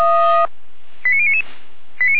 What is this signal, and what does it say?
Five-tone selective call (ZVEI), heard as quick runs of five tones, tones of about 70 ms. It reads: 89099
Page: SDR/ZVEI